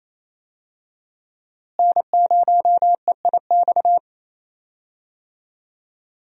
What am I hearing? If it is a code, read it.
Morse code, heard as a keyed tone, about 28 wpm, 700 Hz. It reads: N0EIX
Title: USA Call Signs - 5 Characters - Repeated 3x - 28wpm